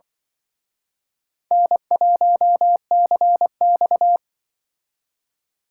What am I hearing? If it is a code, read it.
Morse code, heard as a keyed tone, about 25 wpm, 700 Hz. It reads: N1CX